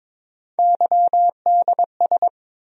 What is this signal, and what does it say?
Morse code, heard as a keyed tone, about 21 wpm, 700 Hz. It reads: YDS
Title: USA Call Sign Postfixes - 3 Letters 22wpm